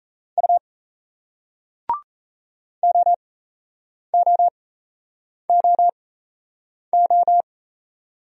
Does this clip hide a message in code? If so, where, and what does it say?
Yes: Morse code from 0.4 to 7.4 s, heard as a keyed tone, about 38 wpm, spaced as for 8 wpm, 700 Hz. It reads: U OOOO